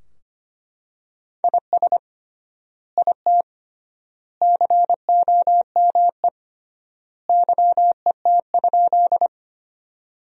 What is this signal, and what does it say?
Morse code, heard as a keyed tone, about 25 wpm, 700 Hz. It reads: IS IT COME YET?
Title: 3x Word-Spacing - Sentences from Top 500 Words - 25wpm